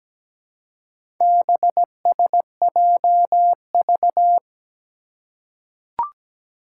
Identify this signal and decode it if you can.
Morse code, heard as a keyed tone, about 17 wpm, 700 Hz. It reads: BSJV
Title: Four Challenging Letter-Number Combinations 17wpm